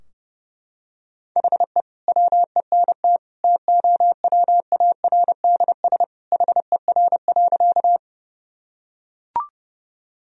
Morse code, heard as a keyed tone, about 30 wpm, 700 Hz. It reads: HE WENT TOWARDS HER.
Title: Sentences from Top 1000 Words - Review Part 2 - 30wpm